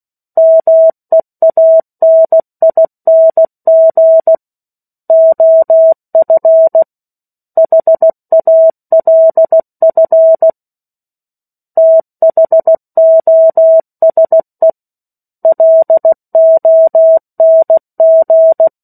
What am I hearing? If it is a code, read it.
Morse code, heard as a keyed tone, about 16 wpm, 650 Hz. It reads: MEANING OF HALF THOSE LONG
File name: 03-Chapter-03 16 WPM